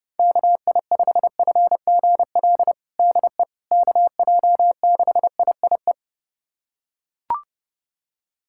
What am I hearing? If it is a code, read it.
Morse code, heard as a keyed tone, about 30 wpm, 700 Hz. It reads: KI5FGL DE KJ6IIE
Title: USA Callsign DE Callsign - 6 Characters - 30wpm